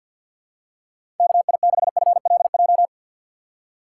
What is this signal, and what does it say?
Morse code, heard as a keyed tone, about 50 wpm, 700 Hz. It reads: KI6FLJ
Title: USA Call Signs - 6 Characters - 4x Character Spacing - 50wpm